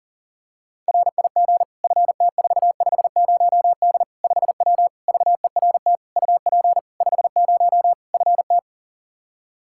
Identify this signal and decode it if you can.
Morse code, heard as a keyed tone, about 40 wpm, 700 Hz. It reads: RIG FT450D 5W VERT UP 50 FT
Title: Practice QSO - Exchange 2 - Info - 40wpm